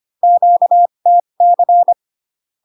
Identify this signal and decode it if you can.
Morse code, heard as a keyed tone, about 25 wpm, spaced as for 22 wpm, 700 Hz. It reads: QTC
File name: Q-codes most common repeated 3 times 18 25wpm